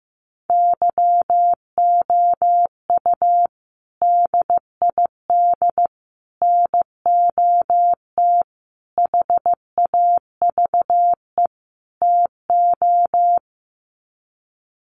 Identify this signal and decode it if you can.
Morse code, heard as a keyed tone, about 15 wpm, 700 Hz. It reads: YOU DID NOT HAVE TO
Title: Sets of 5 Words Encore 15wpm